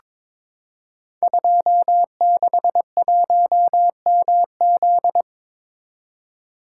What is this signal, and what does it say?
Morse code, heard as a keyed tone, about 23 wpm, 700 Hz. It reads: 261MZ
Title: Code Groups - 5 Letter:Numbers - 22wpm